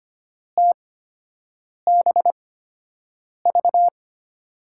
Morse code, heard as a keyed tone, about 25 wpm, spaced as for 7 wpm, 700 Hz. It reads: TBV
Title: USA Call Signs - 5 Characters - 8x Character Spacing - 25wpm